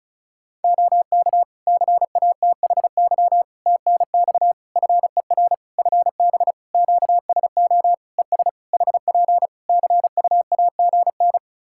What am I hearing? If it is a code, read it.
Morse code, heard as a keyed tone, about 34 wpm, 700 Hz. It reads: OK CATHY TNX FER FB QSO ES HP CUAGN
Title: K1USN SST QSO Practice 35wpm